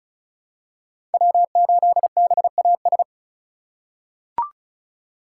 Morse code, heard as a keyed tone, about 34 wpm, 700 Hz. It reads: W8BAS
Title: USA Call Signs - 5 Characters - 4x Character Spacing - 35wpm